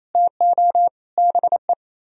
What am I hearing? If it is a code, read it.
Morse code, heard as a keyed tone, about 35 wpm, 700 Hz. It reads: TO BE
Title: Speed-Racing - Sets of 3 Words 28wpm